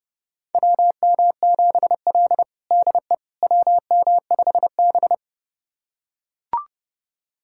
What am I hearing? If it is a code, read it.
Morse code, heard as a keyed tone, about 30 wpm, 700 Hz. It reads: WM7L DE WM5B
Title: USA Callsign DE Callsign - 4 Characters - 30wpm